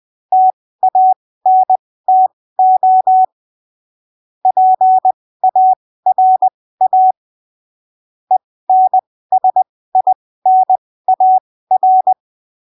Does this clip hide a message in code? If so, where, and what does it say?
Yes: Morse code from 0.3 to 12.1 s, heard as a keyed tone, about 20 wpm, spaced as for 15 wpm, 750 Hz. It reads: TANTO PARA ENSINAR